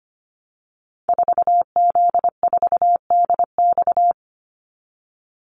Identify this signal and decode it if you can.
Morse code, heard as a keyed tone, about 25 wpm, 700 Hz. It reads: 4Z4DX
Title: Call Signs Repeated 2x - 25wpm